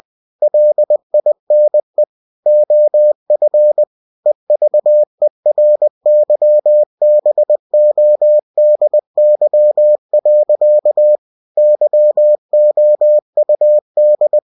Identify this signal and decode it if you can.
Morse code, heard as a keyed tone, about 20 wpm, 600 Hz. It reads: LINE OF EVERYBODY. YOUD
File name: cw-books / winnie-the-pooh / winnie-20wpm-page-0077